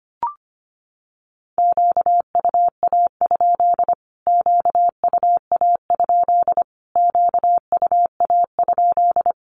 Morse code, heard as a keyed tone, about 25 wpm, 700 Hz. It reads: QUA? QUA? QUA?
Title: Q-Codes Repeated 3x 25wpm